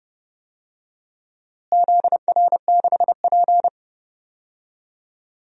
Morse code, heard as a keyed tone, about 30 wpm, 700 Hz. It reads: ZR6P
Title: DXCC-100 Call Signs - 4 Characters - Repeated 3x - 30wpm